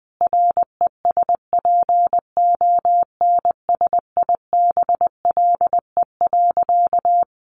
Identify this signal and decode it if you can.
Morse code, heard as a keyed tone, about 20 wpm, 700 Hz. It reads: RESPONSIBLE.
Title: A Sound Of Thunder 20wpm